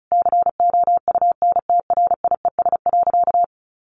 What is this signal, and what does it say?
Morse code, heard as a keyed tone, about 36 wpm, 700 Hz. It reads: COUNTRIES.
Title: A Sound Of Thunder 35wpm